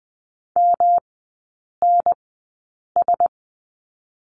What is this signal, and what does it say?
Morse code, heard as a keyed tone, about 20 wpm, spaced as for 8 wpm, 700 Hz. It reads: MNS
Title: 107 Lesson 18e - Letter m - Three Characters 20wpm